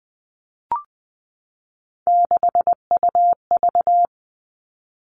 Morse code, heard as a keyed tone, about 20 wpm, 700 Hz. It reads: 6UV
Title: All Cut Numbers - 3-Digit 20wpm